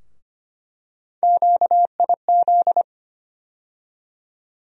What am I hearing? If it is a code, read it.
Morse code, heard as a keyed tone, about 25 wpm, 700 Hz. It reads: QIZ
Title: USA Call Sign Postfixes - 3 Letters 25wpm